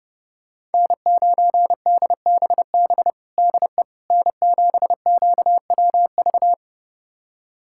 Morse code, heard as a keyed tone, about 30 wpm, 700 Hz. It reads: N9DBB DE N7QWV
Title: USA Callsign DE Callsign - 5 Characters - 30wpm